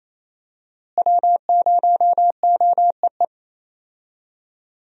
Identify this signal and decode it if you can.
Morse code, heard as a keyed tone, about 28 wpm, 700 Hz. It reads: W0OEE
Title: USA Call Signs - 5 Characters - Repeated 3x - 28wpm